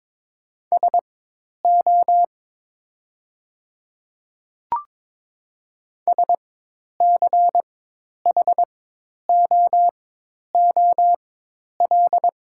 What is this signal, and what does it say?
Morse code, heard as a keyed tone, about 22 wpm, spaced as for 10 wpm, 700 Hz. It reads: SO SCHOOL
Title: 4x Character Spacing - Top 500 Words - 22wpm